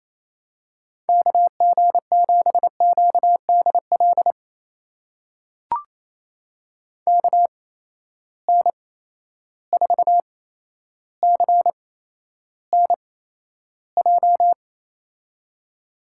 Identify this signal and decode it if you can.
Morse code, heard as a keyed tone, about 28 wpm, spaced as for 12 wpm, 700 Hz. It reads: KG7QDL KN4CNJ
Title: USA Call Signs - 6 Characters - 8x Character Spacing - 28wpm